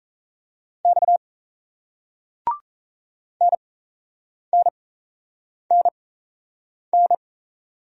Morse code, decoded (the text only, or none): X NNNN